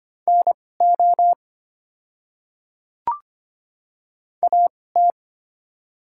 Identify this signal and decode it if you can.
Morse code, heard as a keyed tone, about 25 wpm, spaced as for 18 wpm, 700 Hz. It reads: NO AT
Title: 2x Character Spacing - Top 100 Words - 25wpm